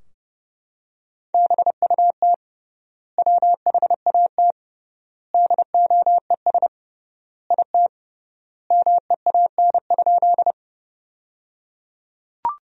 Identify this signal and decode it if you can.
Morse code, heard as a keyed tone, about 30 wpm, 700 Hz. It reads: BUT WHAT DOES IT MEAN?
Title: 3x Word-Spacing - Sentences from Top 200 Words - 30wpm